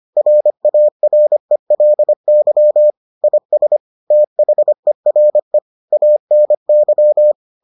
Morse code, heard as a keyed tone, about 25 wpm, 600 Hz. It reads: RARELY IS THERE ANY